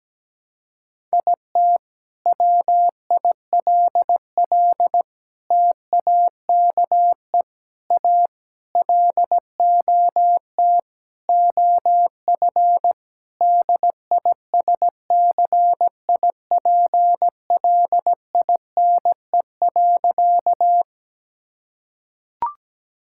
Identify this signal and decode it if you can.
Morse code, heard as a keyed tone, about 17 wpm, 700 Hz. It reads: IT WILL TAKE A LOT OF DISCIPLINE.
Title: Sentences from Top 1000 Words - Review Part 6 - 17wpm